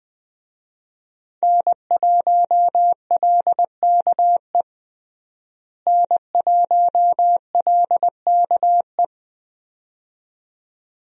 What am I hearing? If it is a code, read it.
Morse code, heard as a keyed tone, about 21 wpm, 700 Hz. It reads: N1LKE N1LKE
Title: USA Call Signs - 5 Characters - Repeated 2x - 20wpm